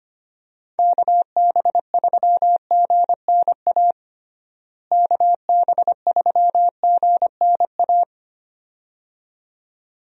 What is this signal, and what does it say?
Morse code, heard as a keyed tone, about 25 wpm, 700 Hz. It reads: KB3GNA KB3GNA
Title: USA Call Signs - 6 Characters - Repeated 2x - 25wpm